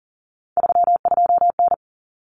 Morse code, heard as a keyed tone, about 40 wpm, 700 Hz. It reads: 32N
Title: Common Cut Numbers - 3-Digit 40wpm